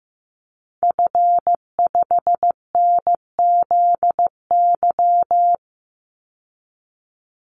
Morse code, heard as a keyed tone, about 15 wpm, 700 Hz. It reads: F5NZY
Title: Call Signs Repeated 2x - 15wpm